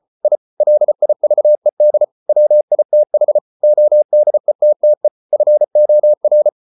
Morse code, heard as a keyed tone, about 34 wpm, 600 Hz. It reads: I LIVED WITH ODETTE FOR